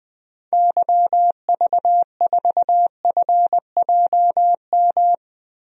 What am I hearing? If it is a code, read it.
Morse code, heard as a keyed tone, about 21 wpm, 700 Hz. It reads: YV4FJM
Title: DXCC-100 Call Signs - 6 Characters - Repeated 2x - 20wpm